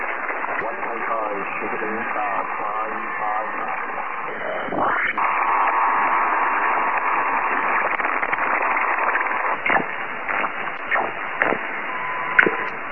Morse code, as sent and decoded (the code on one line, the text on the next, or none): none